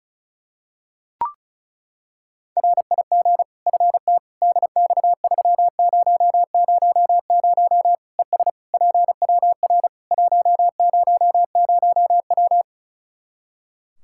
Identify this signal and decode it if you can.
Morse code, heard as a keyed tone, about 36 wpm, 700 Hz. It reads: RIG FT DX3000 ES PWR 100W
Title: Practice QSO - Exchange 2 - Info - 35wpm